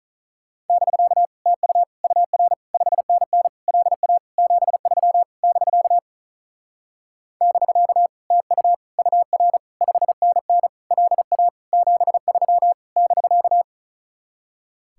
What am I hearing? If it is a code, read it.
Morse code, heard as a keyed tone, about 45 wpm, 700 Hz. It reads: <BK> TU UR 5NN LA 73 <BK> <BK> TU UR 5NN LA 73 <BK>